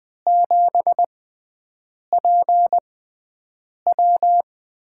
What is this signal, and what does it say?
Morse code, heard as a keyed tone, about 21 wpm, spaced as for 7 wpm, 700 Hz. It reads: 7PW
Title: USA Call Signs - 5 Characters - 6x Character Spacing - 20wpm